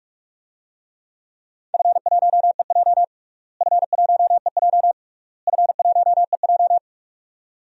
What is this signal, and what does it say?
Morse code, heard as a keyed tone, about 46 wpm, 690 Hz. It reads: F1EJ F1EJ F1EJ
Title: DXCC-100 Call Signs - 4 Characters - Repeated 3x - 45wpm